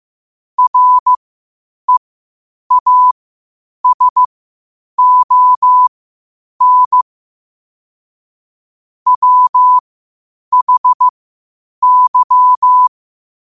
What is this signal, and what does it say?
Morse code, heard as a keyed tone, about 15 wpm, spaced as for 8 wpm, 1000 Hz. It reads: REASON WHY
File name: qotdfiles / 8wpm